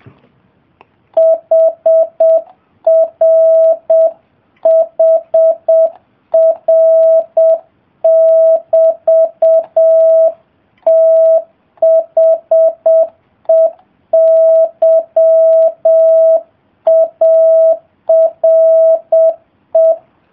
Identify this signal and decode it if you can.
Morse code, heard as a keyed tone, about 7 wpm, 660 Hz. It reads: HRHR<BT>THEY ARE